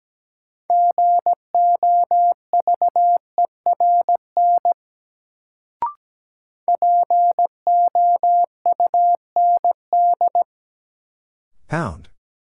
Morse code, heard as a keyed tone, about 17 wpm, 700 Hz. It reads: GOVERN POUND
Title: Sentences from Top 1000 Words - Review Part 2 - New Word Drill - 17wpm